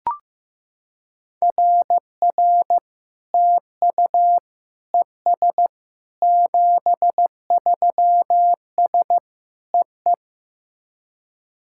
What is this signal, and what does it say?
Morse code, heard as a keyed tone, about 16 wpm, 700 Hz. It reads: RR TU ES 73S EE
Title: SOTA QSO Practice 15wpm